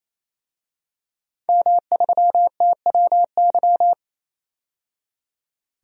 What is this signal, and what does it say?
Morse code, heard as a keyed tone, about 28 wpm, 700 Hz. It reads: M3TWY
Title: DXCC-100 Call Signs - 5 Characters - 28wpm